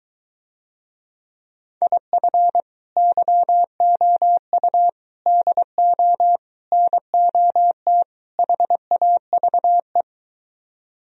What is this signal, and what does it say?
Morse code, heard as a keyed tone, about 23 wpm, 700 Hz. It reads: IF YOU DO NOT HAVE